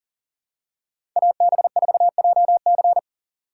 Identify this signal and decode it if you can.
Morse code, heard as a keyed tone, about 40 wpm, 700 Hz. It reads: AB4JC
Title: USA Call Signs - 3x Character Spacing - 40wpm